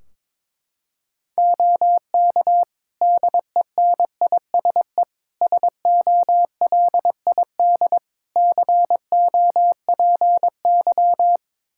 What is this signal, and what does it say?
Morse code, heard as a keyed tone, about 22 wpm, 700 Hz. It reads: OK DENISE SOLID COPY
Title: K1USN SST QSO Practice 22wpm